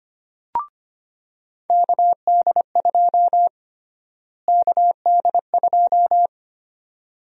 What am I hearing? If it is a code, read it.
Morse code, heard as a keyed tone, about 25 wpm, 700 Hz. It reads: KD2 KD2
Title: USA Call Sign Prefixes - Repeated 2x - 25wpm